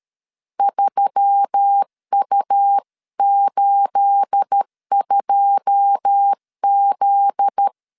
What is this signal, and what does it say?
Morse code, heard as a keyed tone, about 13 wpm, 800 Hz. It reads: 3U82Z
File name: CW / DET / 1 9 1500C S C 12wpm 3811